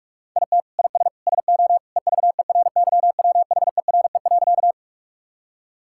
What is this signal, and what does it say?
Morse code, heard as a keyed tone, about 46 wpm, 700 Hz. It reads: IT IS SO EVERYWHERE.
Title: Sentences from Top 1600 Words 45wpm